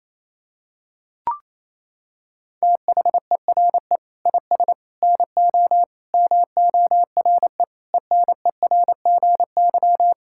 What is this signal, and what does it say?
Morse code, heard as a keyed tone, about 28 wpm, 700 Hz. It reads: THERE IS NO MORE ENERGY
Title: Sentences from Top 1000 Words - Review Part 6 - 28wpm